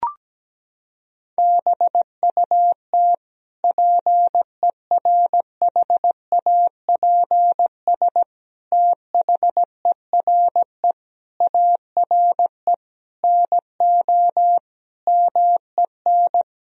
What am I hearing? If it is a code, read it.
Morse code, heard as a keyed tone, about 17 wpm, 700 Hz. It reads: BUT PERHAPS THERE ARE NO MEN
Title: Sentences from Top 1000 Words - Review Part 3 - 17wpm